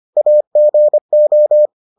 Morse code, heard as a keyed tone, about 25 wpm, 600 Hz. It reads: AGO